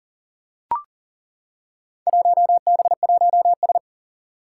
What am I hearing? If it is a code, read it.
Morse code, heard as a keyed tone, about 37 wpm, 700 Hz. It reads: 1B1S
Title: Four Challenging Letter-Number Combinations 40wpm